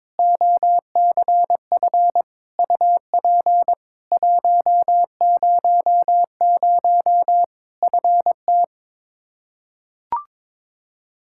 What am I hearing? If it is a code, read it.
Morse code, heard as a keyed tone, about 22 wpm, 700 Hz. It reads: OCF UP 100 FT
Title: Practice QSO - Exchange 2 - Info - 22wpm